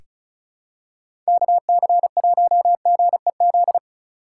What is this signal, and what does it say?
Morse code, heard as a keyed tone, about 36 wpm, 700 Hz. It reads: KC1GEZ
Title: USA Call Signs - 6 Characters - 3x Character Spacing - 35wpm